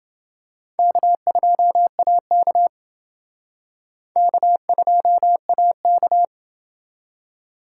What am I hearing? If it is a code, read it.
Morse code, heard as a keyed tone, about 29 wpm, 700 Hz. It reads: K2AK K2AK